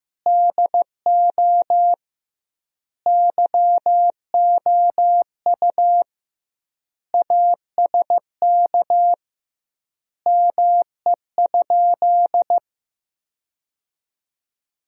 Morse code, heard as a keyed tone, about 15 wpm, 700 Hz. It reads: DO YOU ASK ME?
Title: 2x Word-Spacing - Sentences from Top 200 Words - 15wpm